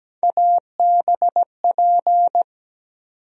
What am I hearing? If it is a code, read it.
Morse code, heard as a keyed tone, about 17 wpm, 700 Hz. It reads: ABP
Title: USA Call Sign Postfixes - 3 Letters 17wpm